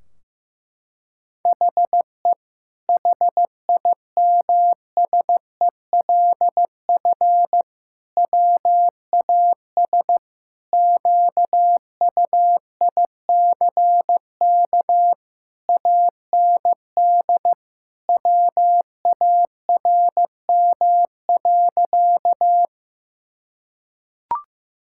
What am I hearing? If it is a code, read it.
Morse code, heard as a keyed tone, about 16 wpm, 700 Hz. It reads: HE HIMSELF WAS QUICK AND WARM.
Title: Sentences from Top 1600 Words 15wpm